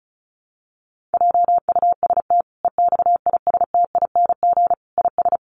Morse code, heard as a keyed tone, about 34 wpm, 700 Hz. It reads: JUST EXISTING IS
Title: Sentences from Top 1500 Words 35wpm